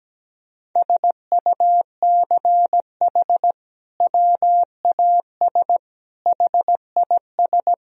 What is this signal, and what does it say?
Morse code, heard as a keyed tone, about 17 wpm, 700 Hz. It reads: SUCH WAS HIS
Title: Sentences from Top 1000 Words - Review Part 4 - 17wpm